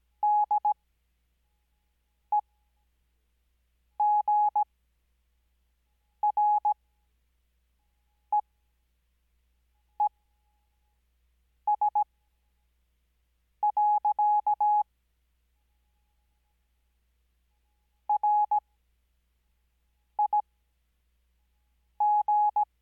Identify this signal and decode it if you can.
Morse code, heard as a keyed tone, about 17 wpm, spaced as for 5 wpm, 840 Hz. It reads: DEGREES. RIG
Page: Farnsworth Spacing: 15WPM Character speed Spaced at 5WPM.